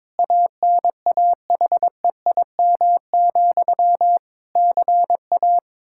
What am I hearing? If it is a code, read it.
Morse code, heard as a keyed tone, about 23 wpm, 700 Hz. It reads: ANAHEIM, CA
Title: Practice QSO - Exchange 1 - Beginning - 22wpm